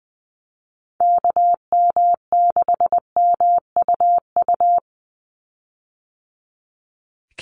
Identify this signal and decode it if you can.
Morse code, heard as a keyed tone, about 20 wpm, 700 Hz. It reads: KM6MUU
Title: USA Call Signs 20wpm